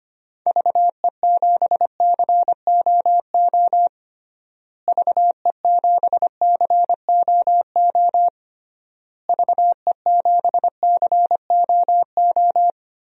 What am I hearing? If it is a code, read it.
Morse code, heard as a keyed tone, about 25 wpm, 700 Hz. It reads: VE7COO VE7COO VE7COO
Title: DXCC-100 Call Signs - 6 Characters - Repeated 3x - 25wpm